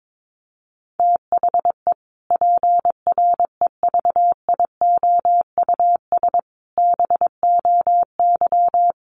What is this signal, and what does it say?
Morse code, heard as a keyed tone, about 22 wpm, 700 Hz. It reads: THE PREVIOUS BOY
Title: Sentences from Top 1500 Words 22wpm